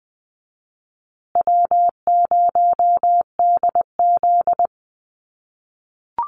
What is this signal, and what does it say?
Morse code, heard as a keyed tone, about 20 wpm, 700 Hz. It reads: W0DZ
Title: USA Call Signs Repeated 2x 20wpm